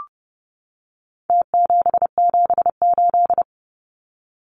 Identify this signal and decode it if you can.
Morse code, heard as a keyed tone, about 30 wpm, 700 Hz. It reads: T778
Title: Numbers - Contest Style - 4-Digit 30wpm